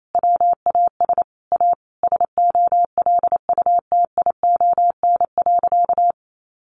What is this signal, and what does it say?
Morse code, heard as a keyed tone, about 28 wpm, 700 Hz. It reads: WAS A SOLUTION.
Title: Sentences from Top 1300 Words 28wpm